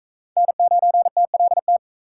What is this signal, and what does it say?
Morse code, heard as a keyed tone, about 44 wpm, 700 Hz. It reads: N9TLT